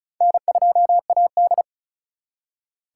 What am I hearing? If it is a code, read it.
Morse code, heard as a keyed tone, about 47 wpm, spaced as for 42 wpm, 700 Hz. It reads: N2AD
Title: Speed Racing - USA 1x2 Call Signs 35wpm